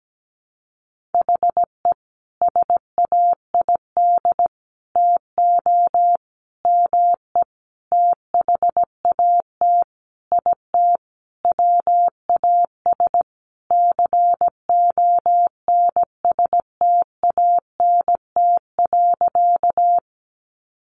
Morse code, heard as a keyed tone, about 17 wpm, 700 Hz. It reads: HE SAID TO ME THAT IT WAS CONSTANT.
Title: Sentences from Top 1500 Words 17wpm